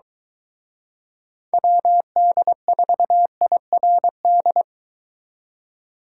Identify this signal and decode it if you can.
Morse code, heard as a keyed tone, about 23 wpm, 700 Hz. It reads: WD4IRD